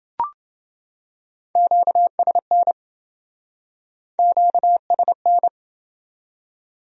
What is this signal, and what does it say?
Morse code, heard as a keyed tone, about 28 wpm, 700 Hz. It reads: QSN QSN